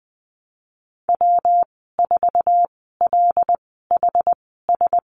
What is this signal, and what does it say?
Morse code, heard as a keyed tone, about 20 wpm, spaced as for 15 wpm, 700 Hz. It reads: W4LHS
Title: 173 Lesson 28g - Number 9 - Call Signs 20wpm